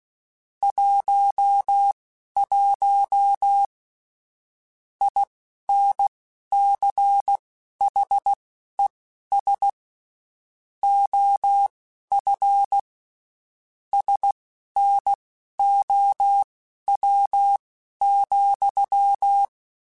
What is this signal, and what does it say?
Morse code, heard as a keyed tone, about 16 wpm, spaced as for 11 wpm, 800 Hz. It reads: 11 INCHES OF SNOW,
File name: (spaced) / QSOs Spaced-mp3s / WA7QOX 1 @15 sp